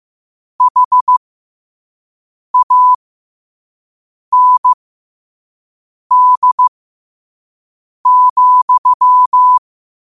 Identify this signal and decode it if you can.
Morse code, heard as a keyed tone, about 15 wpm, spaced as for 5 wpm, 1000 Hz. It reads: HAND,
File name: qotdfiles / 5wpm